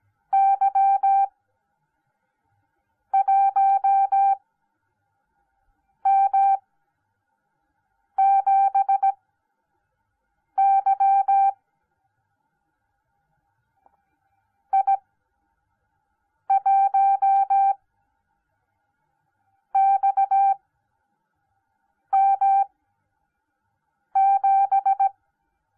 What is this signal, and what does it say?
Morse code, heard as a keyed tone, about 17 wpm, spaced as for 5 wpm, 780 Hz. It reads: Y1M7Y I1XM7